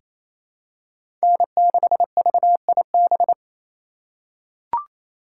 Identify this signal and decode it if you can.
Morse code, heard as a keyed tone, about 27 wpm, 700 Hz. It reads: N6VIB
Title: USA Call Signs - 4x Character Spacing - 28wpm